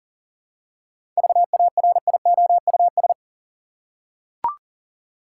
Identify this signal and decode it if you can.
Morse code, heard as a keyed tone, about 40 wpm, 700 Hz. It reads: VARIOUS
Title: Seven Letter Words - 40wpm